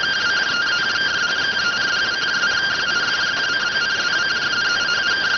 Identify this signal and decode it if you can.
Two-tone teleprinter signal, about 170 Hz shift.